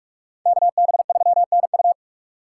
Morse code, heard as a keyed tone, about 44 wpm, 700 Hz. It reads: KB3NU